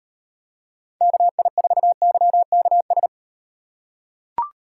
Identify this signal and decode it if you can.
Morse code, heard as a keyed tone, about 38 wpm, 700 Hz. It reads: KI4YKS